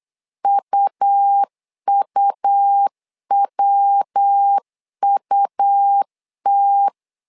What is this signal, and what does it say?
Morse code, heard as a keyed tone, about 8 wpm, 800 Hz. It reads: UUWUT
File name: LEZIONI / 3 MWRO / 3 MW 8wpm 2501